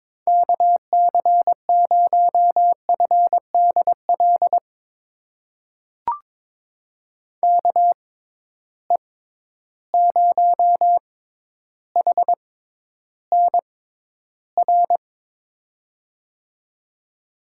Morse code, heard as a keyed tone, about 22 wpm, spaced as for 11 wpm, 700 Hz. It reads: KC0FDL KE0HNR
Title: USA Call Signs - 6 Characters - 6x Character Spacing - 22wpm